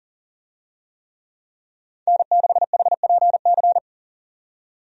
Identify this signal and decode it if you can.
Morse code, heard as a keyed tone, about 37 wpm, 700 Hz. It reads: N6HPC
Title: USA Call Signs - 5 Characters - 40wpm